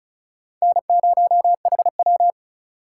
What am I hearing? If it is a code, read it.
Morse code, heard as a keyed tone, about 44 wpm, 700 Hz. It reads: N0HW